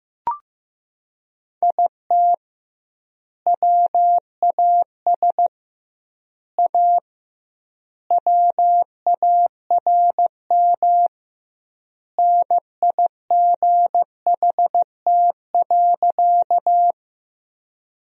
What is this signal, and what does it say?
Morse code, heard as a keyed tone, about 15 wpm, 700 Hz. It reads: IT WAS A WARM NIGHT.
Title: 2x Word-Spacing - Sentences from Top 500 Words - 15wpm